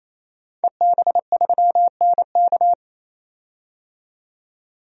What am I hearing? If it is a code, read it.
Morse code, heard as a keyed tone, about 28 wpm, 700 Hz. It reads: EB3NK
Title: DXCC-100 Call Signs - 5 Characters - 28wpm